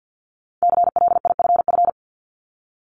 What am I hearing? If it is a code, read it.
Morse code, heard as a keyed tone, about 50 wpm, 700 Hz. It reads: CLIFF
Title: Multi-letter Phonemes 50wpm